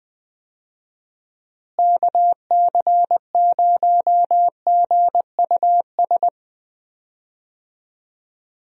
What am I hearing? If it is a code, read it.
Morse code, heard as a keyed tone, about 20 wpm, 700 Hz. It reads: KC0GUS
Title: USA Call Signs - 6 Characters - Repeated 2x - 20wpm